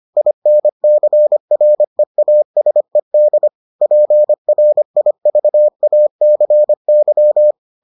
Morse code, heard as a keyed tone, about 25 wpm, 600 Hz. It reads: INCREASED PRIVACY